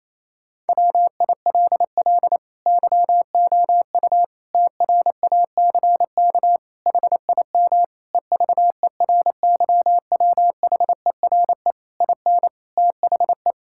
Morse code, heard as a keyed tone, about 28 wpm, 700 Hz. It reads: WILL YOU TRACK HIM EVERYWHERE IN THE
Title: Sentences from Top 1600 Words 28wpm